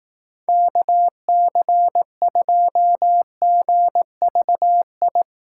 Morse code, heard as a keyed tone, about 18 wpm, 700 Hz. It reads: KC2GVI
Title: Speed Racing - USA 2x3 Call Signs 15wpm